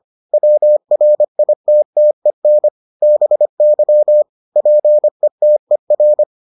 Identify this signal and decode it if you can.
Morse code, heard as a keyed tone, about 25 wpm, 600 Hz. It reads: WRITTEN BY PETER